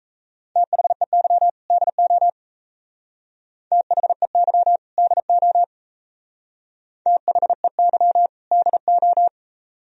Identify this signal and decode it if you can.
Morse code, heard as a keyed tone, about 35 wpm, 700 Hz. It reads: THEY DO THEY DO THEY DO